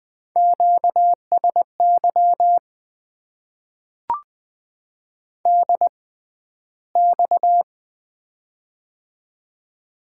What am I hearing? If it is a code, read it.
Morse code, heard as a keyed tone, about 20 wpm, spaced as for 12 wpm, 700 Hz. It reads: QSY DX
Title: Top 90 QSO Elements - 6x Character Spacing - 20wpm